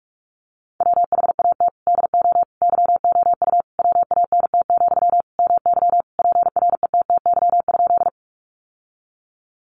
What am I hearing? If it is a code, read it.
Morse code, heard as a keyed tone, about 44 wpm, 700 Hz. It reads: WHAT DO YOU WANT, MY PRETTY?
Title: Sentences from Top 1000 Words 45wpm